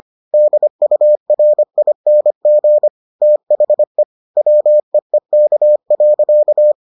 Morse code, heard as a keyed tone, about 25 wpm, 600 Hz. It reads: DURING THE WEEK.